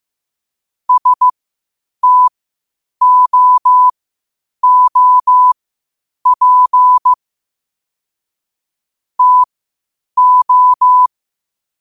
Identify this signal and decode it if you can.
Morse code, heard as a keyed tone, about 15 wpm, spaced as for 8 wpm, 1000 Hz. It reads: STOOP TO
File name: qotdfiles / 8wpm